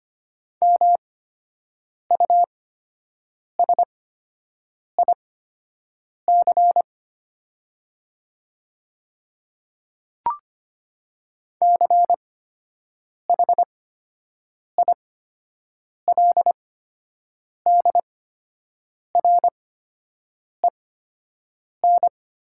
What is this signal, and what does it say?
Morse code, heard as a keyed tone, about 25 wpm, spaced as for 7 wpm, 700 Hz. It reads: MUSIC CHILDREN
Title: 8x Character Spacing - Top 300 Words - 25wpm